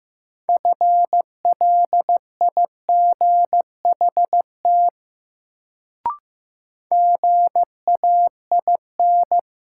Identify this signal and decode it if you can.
Morse code, heard as a keyed tone, about 15 wpm, 700 Hz. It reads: FLIGHT GAIN
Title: Sentences from Top 1000 Words - New Word Drill - 15wpm